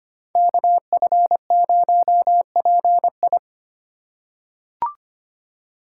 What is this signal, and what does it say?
Morse code, heard as a keyed tone, about 25 wpm, 700 Hz. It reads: KF0PI
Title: USA Call Signs - 5 Characters - 2x Character Spacing - 25wpm